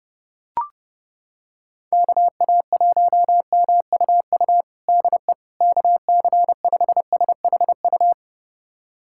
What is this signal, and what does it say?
Morse code, heard as a keyed tone, about 30 wpm, 700 Hz. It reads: KA1MUU DE KC5SHU
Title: USA Callsign DE Callsign - 6 Characters - 30wpm